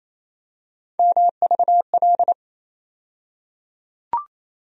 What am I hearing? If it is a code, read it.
Morse code, heard as a keyed tone, about 28 wpm, 700 Hz. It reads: MVL
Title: USA Call Sign Postfixes - 3 Letters 28wpm